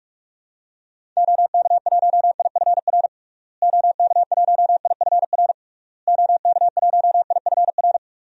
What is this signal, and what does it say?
Morse code, heard as a keyed tone, about 46 wpm, 700 Hz. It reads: OK1IFR OK1IFR OK1IFR
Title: DXCC-100 Call Signs - 6 Characters - Repeated 3x - 45wpm